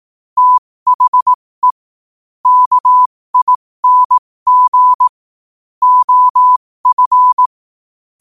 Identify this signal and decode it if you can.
Morse code, heard as a keyed tone, about 18 wpm, spaced as for 15 wpm, 1000 Hz. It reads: THE KING OF
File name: qotdfiles / 15wpm